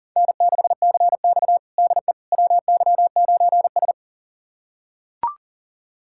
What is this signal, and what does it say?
Morse code, heard as a keyed tone, about 40 wpm, 700 Hz. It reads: N6CX DE WY9S
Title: USA Callsign DE Callsign - 4 Characters - 40wpm